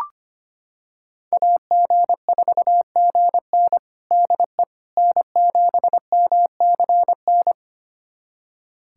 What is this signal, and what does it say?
Morse code, heard as a keyed tone, about 25 wpm, 700 Hz. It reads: AG4GN DE N7MCN
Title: USA Callsign DE Callsign - 5 Characters - 25wpm